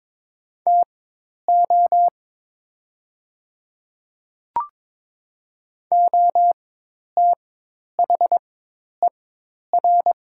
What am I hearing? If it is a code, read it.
Morse code, heard as a keyed tone, about 22 wpm, spaced as for 10 wpm, 700 Hz. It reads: TO OTHER
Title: 4x Character Spacing - Top 100 Words - 22wpm